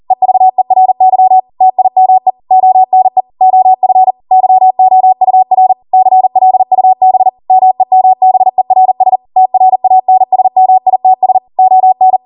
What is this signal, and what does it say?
Morse code, heard as a keyed tone, about 40 wpm, 750 Hz. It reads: EVERY TIME ONE OF YOUR CLUB MEMBERS TRANSMITS ON
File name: Morse / 40WPM